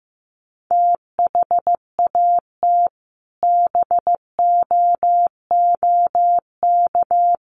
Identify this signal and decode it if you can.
Morse code, heard as a keyed tone, about 14 wpm, 700 Hz. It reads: THAT BOOK
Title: Sentences from Top 1500 Words 15wpm